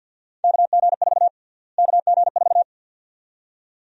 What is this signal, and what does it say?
Morse code, heard as a keyed tone, about 49 wpm, 700 Hz. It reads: KG4 KG4
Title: USA Call Sign Prefixes - Repeated 2x - 50wpm